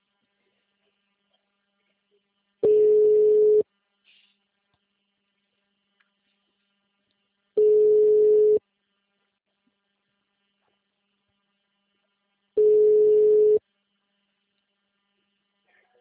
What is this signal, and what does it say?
Ringback tone (Europe, 425 Hz).